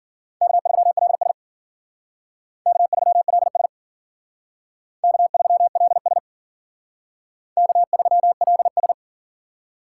Morse code, heard as a keyed tone, about 51 wpm, 700 Hz. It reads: K3LS K3LS K3LS K3LS